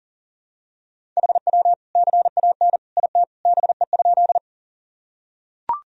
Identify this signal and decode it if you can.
Morse code, heard as a keyed tone, about 40 wpm, 700 Hz. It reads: HW CAN IT BE?
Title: Sentences from Top 100 Words and Abbreviations 40wpm